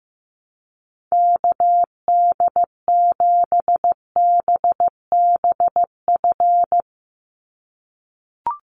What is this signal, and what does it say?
Morse code, heard as a keyed tone, about 15 wpm, 700 Hz. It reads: KD7BBF
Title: USA Call Signs 15wpm